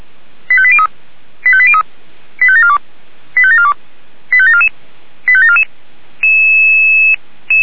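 Five-tone selective call (ZVEI), heard as quick runs of five tones, tones of about 70 ms. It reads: 85803 85803 85742 85742 85744 85744
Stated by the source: ZVEI five-tone alert received by SDR.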